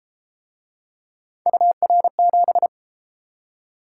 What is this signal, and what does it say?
Morse code, heard as a keyed tone, about 30 wpm, 700 Hz. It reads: UR7